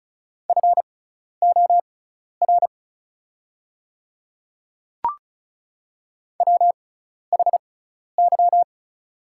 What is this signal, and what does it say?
Morse code, heard as a keyed tone, about 34 wpm, spaced as for 12 wpm, 690 Hz. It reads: FOR WHY
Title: 6x Character Spacing - Top 200 Words - 35wpm